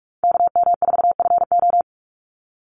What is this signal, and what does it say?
Morse code, heard as a keyed tone, about 44 wpm, 700 Hz. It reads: KM4FO
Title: Call Signs Repeated 2x - 45wpm